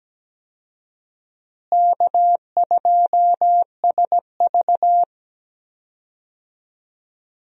Morse code, heard as a keyed tone, about 17 wpm, 700 Hz. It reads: K2SV
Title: USA Call Signs - 4 Characters - Repeated 3x - 17wpm